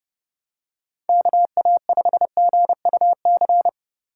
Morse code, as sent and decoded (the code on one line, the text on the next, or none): -.- .- ..... --. ..- -.-.
KA5GUC